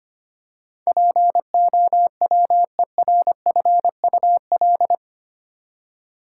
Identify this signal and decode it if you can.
Morse code, heard as a keyed tone, about 25 wpm, 700 Hz. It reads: POWERFUL
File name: 8 Letter Words - First 3 Letters Given - 25wpm